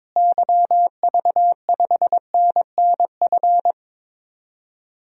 Morse code, heard as a keyed tone, about 22 wpm, 700 Hz. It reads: YV5NNF
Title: DXCC-100 Call Signs - 6 Characters - Repeated 3x - 22wpm